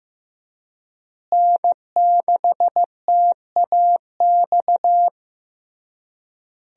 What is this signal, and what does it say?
Morse code, heard as a keyed tone, about 15 wpm, 700 Hz. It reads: N6TAX
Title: USA Call Signs - 5 Characters - Repeated 3x - 15wpm